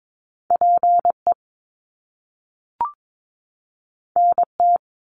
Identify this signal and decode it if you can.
Morse code, heard as a keyed tone, about 22 wpm, 700 Hz. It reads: PE NT
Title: Canadian Provinces and Territories 22wpm